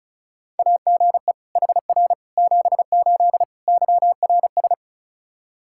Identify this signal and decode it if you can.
Morse code, heard as a keyed tone, about 36 wpm, 700 Hz. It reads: AGE HR 78 YRS
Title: Practice QSO - Exchange 3 - More Info - 35wpm